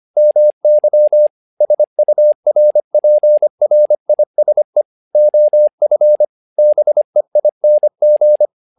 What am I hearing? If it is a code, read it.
Morse code, heard as a keyed tone, about 25 wpm, 600 Hz. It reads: MY SURPRISE OF BEING
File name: learning center / tower trouble / tower trouble 25wpm